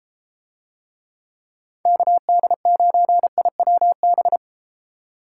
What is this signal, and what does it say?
Morse code, heard as a keyed tone, about 32 wpm, 700 Hz. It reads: KD9IWB